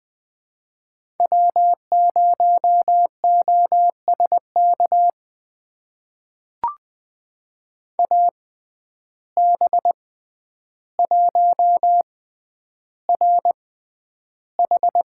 Morse code, heard as a keyed tone, about 20 wpm, spaced as for 10 wpm, 700 Hz. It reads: W0OSK AB1RH
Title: USA Call Signs - 5 Characters - 6x Character Spacing - 20wpm